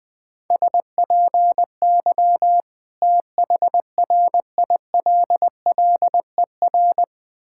Morse code, heard as a keyed tone, about 20 wpm, 700 Hz. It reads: SPY THRILLER
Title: Difficult Words with Hints - 20wpm